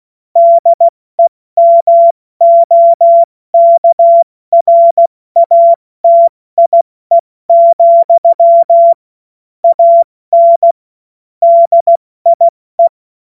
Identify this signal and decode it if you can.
Morse code, heard as a keyed tone, about 16 wpm, spaced as for 14 wpm, 680 Hz. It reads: DEMOKRATIE, AN DIE
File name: Morsekurs / Witze / Witz 09. Februar 2021, 70 F80